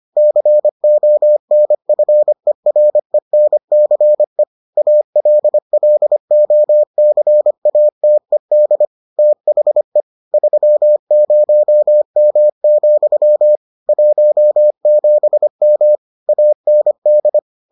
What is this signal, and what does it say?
Morse code, heard as a keyed tone, about 25 wpm, 600 Hz. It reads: CONFERENCE ALLOCATED THE 30M, 17M AND